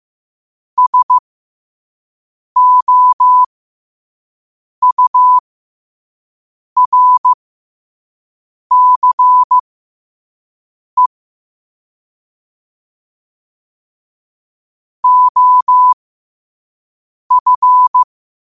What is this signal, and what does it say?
Morse code, heard as a keyed tone, about 15 wpm, spaced as for 5 wpm, 1000 Hz. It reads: SOURCE OF
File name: qotdfiles / 5wpm